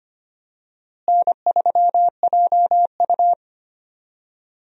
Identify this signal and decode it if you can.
Morse code, heard as a keyed tone, about 25 wpm, 700 Hz. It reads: N3JU
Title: USA Call Signs - 4 Characters - Repeated 3x - 25wpm